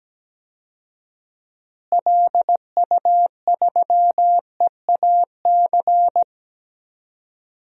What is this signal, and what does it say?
Morse code, heard as a keyed tone, about 17 wpm, 700 Hz. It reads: LU3EAC
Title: DXCC-100 Call Signs - 6 Characters - Repeated 2x - 17wpm